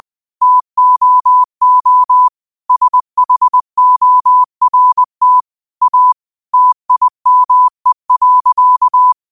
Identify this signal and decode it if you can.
Morse code, heard as a keyed tone, about 20 wpm, 1000 Hz. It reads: TOO SHORT A TIME.